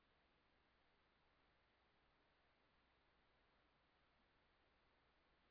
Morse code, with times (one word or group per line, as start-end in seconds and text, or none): none